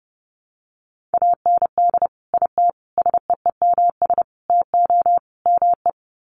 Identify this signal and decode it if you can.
Morse code, heard as a keyed tone, about 28 wpm, 700 Hz. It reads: AND IT SEEMS TO ME
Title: Sets of 5 Words Encore2 30wpm